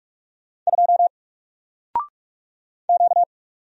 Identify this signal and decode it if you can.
Morse code, heard as a keyed tone, about 42 wpm, 700 Hz. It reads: 2 Q
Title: Single Letter-Number 45wpm